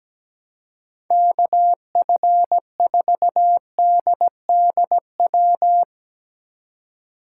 Morse code, heard as a keyed tone, about 17 wpm, 700 Hz. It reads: KF4DDW
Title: USA Call Signs - 6 Characters - 2x Character Spacing - 17wpm